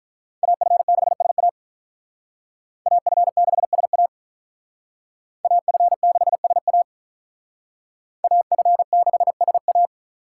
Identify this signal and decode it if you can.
Morse code, heard as a keyed tone, about 45 wpm, 700 Hz. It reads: AF6SA AF6SA AF6SA AF6SA